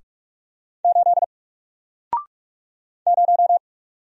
Morse code, heard as a keyed tone, about 46 wpm, 700 Hz. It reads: 8 0
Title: Single Letter-Number 45wpm